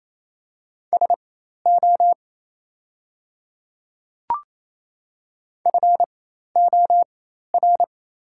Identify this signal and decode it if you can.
Morse code, heard as a keyed tone, about 28 wpm, spaced as for 13 wpm, 700 Hz. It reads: SO FOR
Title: 4x Character Spacing - Top 100 Words - 28wpm